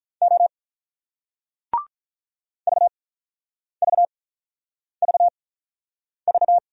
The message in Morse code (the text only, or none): O VVVV